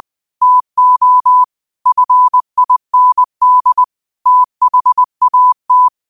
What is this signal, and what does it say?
Morse code, heard as a keyed tone, about 20 wpm, 1000 Hz. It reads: TO FIND THAT
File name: qotdfiles / 20wpm